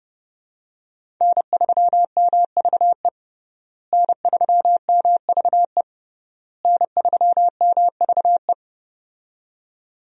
Morse code, heard as a keyed tone, about 32 wpm, 700 Hz. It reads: N3MVE N3MVE N3MVE
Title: USA Call Signs - 5 Characters - Repeated 3x - 30wpm